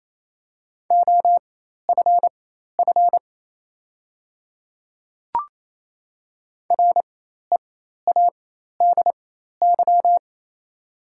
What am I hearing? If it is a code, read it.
Morse code, heard as a keyed tone, about 28 wpm, spaced as for 13 wpm, 700 Hz. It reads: OFF READY
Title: 4x Character Spacing - Top 500 Words - 28wpm